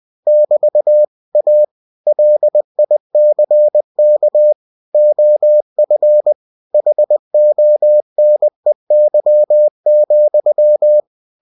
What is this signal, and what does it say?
Morse code, heard as a keyed tone, about 20 wpm, 600 Hz. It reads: <BT> A LICK OF HONEY,
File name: winnie-20wpm